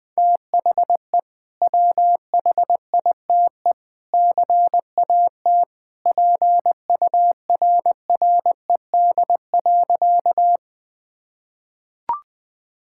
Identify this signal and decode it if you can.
Morse code, heard as a keyed tone, about 20 wpm, 700 Hz. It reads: THE WHITE CAT PURRED.
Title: Sentences with 2-Word Phrases - 20wpm